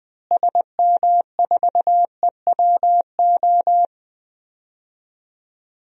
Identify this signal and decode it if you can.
Morse code, heard as a keyed tone, about 20 wpm, 700 Hz. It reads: SM4EWO
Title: DXCC-100 Call Signs - 6 Characters - Repeated 3x - 20wpm